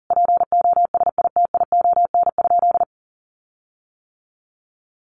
Morse code, heard as a keyed tone, about 40 wpm, 700 Hz. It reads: POSITION?
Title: Sentences from Top 1400 Words 40wpm